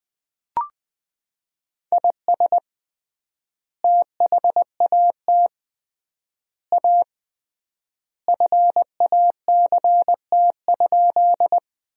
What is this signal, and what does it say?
Morse code, heard as a keyed tone, about 21 wpm, 700 Hz. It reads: IS THAT A FACT?
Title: 3x Word-Spacing - Sentences from Top 500 Words - 20wpm